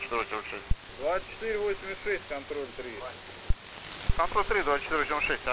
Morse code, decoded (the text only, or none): none